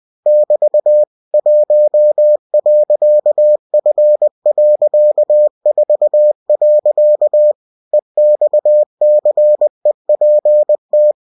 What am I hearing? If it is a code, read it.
Morse code, heard as a keyed tone, about 20 wpm, 600 Hz. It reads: <BT> 1.F.4. EXCEPT
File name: cw-books / winnie-the-pooh / winnie-20wpm-page-0119